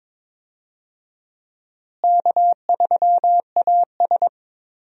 Morse code, heard as a keyed tone, about 22 wpm, 700 Hz. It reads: K3AS
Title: USA Call Signs - 4 Characters - Repeated 2x - 22wpm